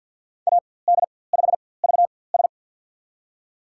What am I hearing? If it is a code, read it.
Morse code, heard as a keyed tone, about 51 wpm, spaced as for 23 wpm, 700 Hz. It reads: AD5VS
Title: USA Call Signs - 4x Character Spacing - 50wpm